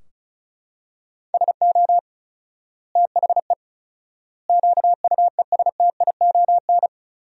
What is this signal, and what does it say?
Morse code, heard as a keyed tone, about 34 wpm, 700 Hz. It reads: SO THE QUESTION
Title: 4x Word-Spacing - Sentences from Top 400 Words - 35wpm